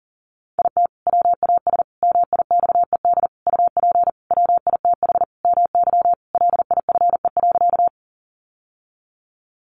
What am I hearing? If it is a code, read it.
Morse code, heard as a keyed tone, about 40 wpm, 700 Hz. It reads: IT WAS MIXED UP WITH MY LIFE.
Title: Sentences from Top 1500 Words 40wpm